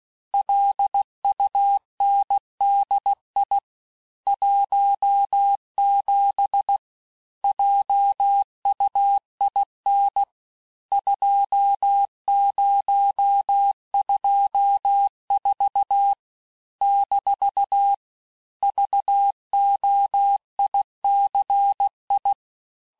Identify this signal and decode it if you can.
Morse code, heard as a keyed tone, about 16 wpm, 800 Hz. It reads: LUNDI 17 JUIN 2024 - VOICI
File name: cours-du-lundi files / Cours-1586-texte-chiffre-vitesse-15mpm